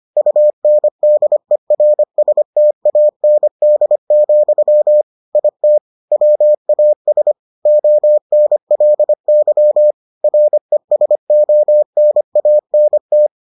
Morse code, heard as a keyed tone, about 25 wpm, 600 Hz. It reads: UNDERSTAND, IT WAS ONLY RESONANT